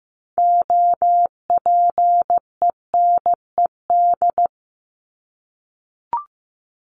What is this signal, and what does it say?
Morse code, heard as a keyed tone, about 15 wpm, 700 Hz. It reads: OPENED
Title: ED Suffix - Top 100 Words with Base Words 15wpm